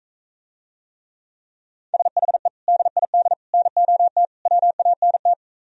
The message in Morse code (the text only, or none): SHE DID NOT WANT